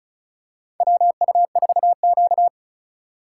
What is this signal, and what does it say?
Morse code, heard as a keyed tone, about 36 wpm, 700 Hz. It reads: WU4Q
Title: USA Call Signs - 4 Characters - 3x Character Spacing - 35wpm